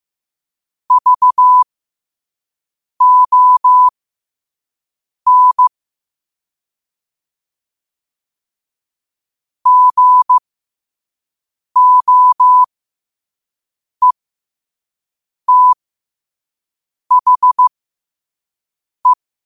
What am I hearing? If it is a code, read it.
Morse code, heard as a keyed tone, about 15 wpm, spaced as for 5 wpm, 1000 Hz. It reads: VON GOETHE
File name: qotdfiles / 5wpm